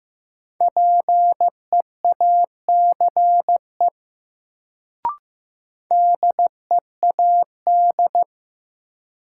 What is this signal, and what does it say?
Morse code, heard as a keyed tone, about 15 wpm, 700 Hz. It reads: PEACE DEAD
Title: Sentences from Top 1500 Words - New Word Drill - 15wpm